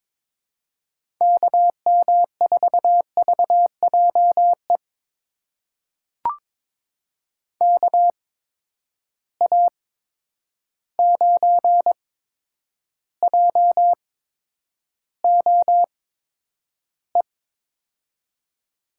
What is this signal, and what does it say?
Morse code, heard as a keyed tone, about 22 wpm, spaced as for 9 wpm, 700 Hz. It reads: KM4VJE KA9JOE
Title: USA Call Signs - 6 Characters - 8x Character Spacing - 22wpm